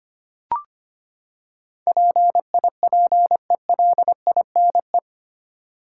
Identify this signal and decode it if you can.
Morse code, heard as a keyed tone, about 25 wpm, 700 Hz. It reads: PIPELINE
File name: Eight Letter Words - 25wpm